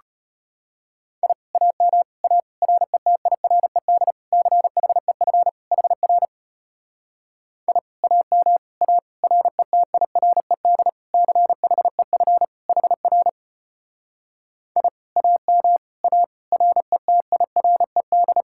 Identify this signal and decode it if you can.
Morse code, heard as a keyed tone, about 33 wpm, 700 Hz. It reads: I AM A RETIRED CHEF HR I AM A RETIRED CHEF HR I AM A RETIRED